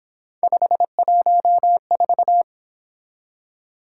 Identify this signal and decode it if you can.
Morse code, heard as a keyed tone, about 27 wpm, 700 Hz. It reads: 514
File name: Speed-Racing - Three Letter-Number Combinations - 22wpm